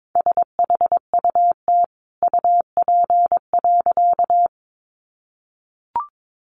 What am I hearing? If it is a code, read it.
Morse code, heard as a keyed tone, about 22 wpm, 700 Hz. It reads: SHUT UP.
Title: A Sound Of Thunder 22wpm